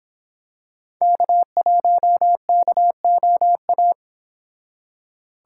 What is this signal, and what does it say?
Morse code, heard as a keyed tone, about 26 wpm, 700 Hz. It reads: K1KOA
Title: Speed Racing - USA 1x3 Call Signs 22wpm